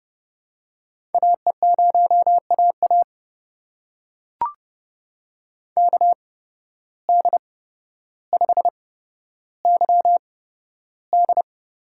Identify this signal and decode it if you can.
Morse code, heard as a keyed tone, about 30 wpm, spaced as for 13 wpm, 700 Hz. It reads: AE0AA KD5YD
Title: USA Call Signs - 8x Character Spacing - 30wpm